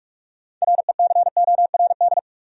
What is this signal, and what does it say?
Morse code, heard as a keyed tone, about 55 wpm, 700 Hz. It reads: RECORD
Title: Speed-Racing Top 500 Words 45wpm